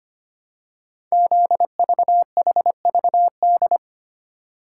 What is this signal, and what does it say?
Morse code, heard as a keyed tone, about 25 wpm, 700 Hz. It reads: ZVHVD
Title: Code Groups - 5 Letters - 25wpm